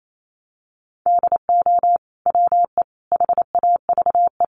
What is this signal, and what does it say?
Morse code, heard as a keyed tone, about 28 wpm, 700 Hz. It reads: DO WE HAVE
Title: Sentences from Top 1400 Words 28wpm